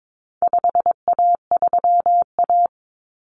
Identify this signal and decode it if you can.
Morse code, heard as a keyed tone, about 22 wpm, 700 Hz. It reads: 5A3A
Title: Common Cut Numbers - 4-Digit 22wpm